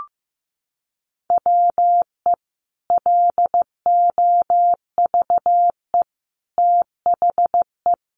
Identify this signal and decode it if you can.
Morse code, heard as a keyed tone, about 15 wpm, 700 Hz. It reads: WE LOVE THE
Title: Sentences from Top 1200 Words 15wpm